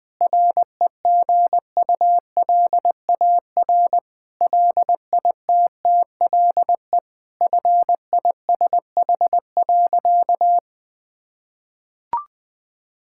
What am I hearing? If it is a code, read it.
Morse code, heard as a keyed tone, about 20 wpm, 700 Hz. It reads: REGULAR LITTLE FISH.
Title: Sentences from Top 1000 Words - Review Part 6 - 20wpm